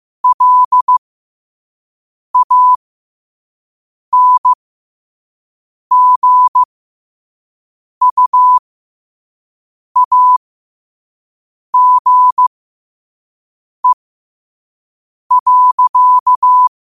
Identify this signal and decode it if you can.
Morse code, heard as a keyed tone, about 15 wpm, spaced as for 5 wpm, 1000 Hz. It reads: LANGUAGE.
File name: qotdfiles / 5wpm